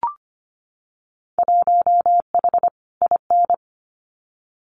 Morse code, heard as a keyed tone, about 25 wpm, 700 Hz. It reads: 1H IN
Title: Winter Field Day Exchanges 25wpm